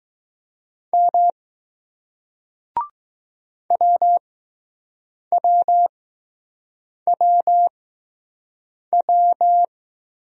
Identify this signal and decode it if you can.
Morse code, heard as a keyed tone, about 20 wpm, spaced as for 6 wpm, 700 Hz. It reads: M WWWW